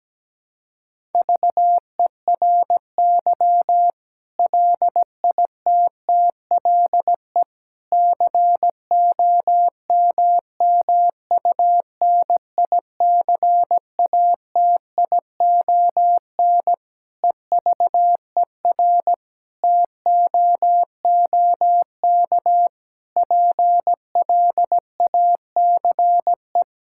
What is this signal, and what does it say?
Morse code, heard as a keyed tone, about 17 wpm, 700 Hz. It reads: VERY LITTLE COMMUNICATION EVER TOOK PLACE
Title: Sentences from Top 1000 Words - Review Part 6 - 17wpm